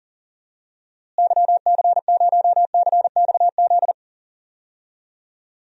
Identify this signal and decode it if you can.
Morse code, heard as a keyed tone, about 40 wpm, 700 Hz. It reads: YC0CXZ
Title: DXCC-100 Call Signs - 6 Characters - Repeated 2x - 40wpm